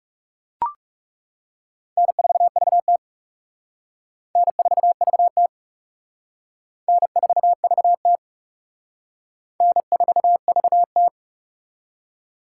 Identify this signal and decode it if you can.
Morse code, heard as a keyed tone, about 40 wpm, 700 Hz. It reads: N4VT N4VT N4VT N4VT